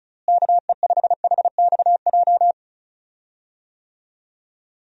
Morse code, heard as a keyed tone, about 34 wpm, 700 Hz. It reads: KE5HXJ
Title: USA Call Signs - 6 Characters - Repeated 2x - 35wpm